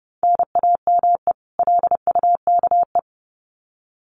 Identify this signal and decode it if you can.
Morse code, heard as a keyed tone, about 30 wpm, 700 Hz. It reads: NAME LUKE
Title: Common QSO Phrases 30wpm